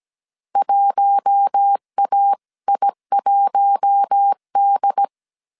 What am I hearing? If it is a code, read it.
Morse code, heard as a keyed tone, about 17 wpm, 800 Hz. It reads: 1AI1D
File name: 16wpm